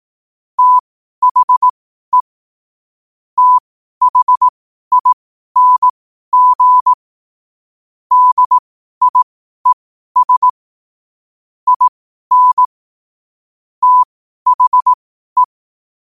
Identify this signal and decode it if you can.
Morse code, heard as a keyed tone, about 18 wpm, spaced as for 12 wpm, 1000 Hz. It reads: THE THING DIES IN THE
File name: 2025-11-08-12wpm